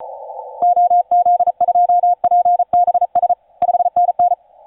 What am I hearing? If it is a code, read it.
Morse code, heard as a keyed tone, about 36 wpm, 710 Hz. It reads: OZ2PBS 5NN
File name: 5u5r 17m cw